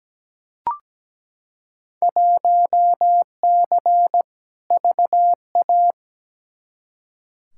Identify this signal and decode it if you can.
Morse code, heard as a keyed tone, about 17 wpm, 700 Hz. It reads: 1C VA
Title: Field Day Exchanges Encore - 17wpm